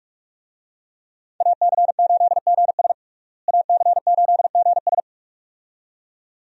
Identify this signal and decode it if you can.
Morse code, heard as a keyed tone, about 46 wpm, 700 Hz. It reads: AC8GS AC8GS
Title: USA Call Signs - 5 Characters - Repeated 2x - 45wpm